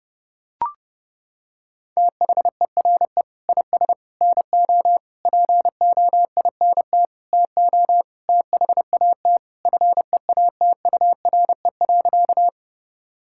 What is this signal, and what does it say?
Morse code, heard as a keyed tone, about 30 wpm, 700 Hz. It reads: THERE IS NO POINT TO THAT FEATURE.
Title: Sentences from Top 1000 Words - Review Part 5 - 30wpm